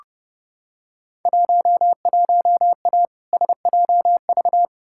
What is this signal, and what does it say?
Morse code, heard as a keyed tone, about 30 wpm, 700 Hz. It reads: 11A SJV
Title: Field Day Exchanges Encore - 30wpm